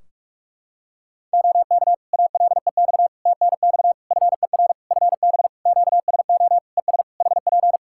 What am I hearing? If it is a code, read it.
Morse code, heard as a keyed tone, about 46 wpm, 700 Hz. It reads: OK ALEX TNX FER FB QSO ES HP
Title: Practice QSO - Exchange 4 - Ending - 45wpm